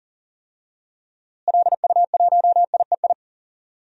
Morse code, heard as a keyed tone, about 40 wpm, 700 Hz. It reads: LU1IEI
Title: DXCC-100 Call Signs - 6 Characters - Repeated 2x - 40wpm